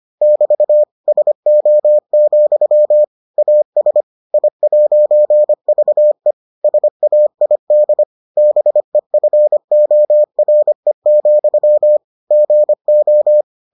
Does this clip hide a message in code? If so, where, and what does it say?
Yes: Morse code from 0.2 to 13.4 s, heard as a keyed tone, about 25 wpm, 600 Hz. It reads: <BT> SO, AS I'VE SAID BEFORE, GO